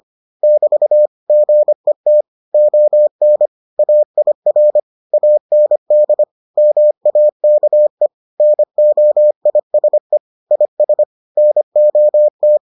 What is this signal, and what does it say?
Morse code, heard as a keyed tone, about 25 wpm, 600 Hz. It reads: <BT> GET ON AIR AND MAKE NOISE IS NOT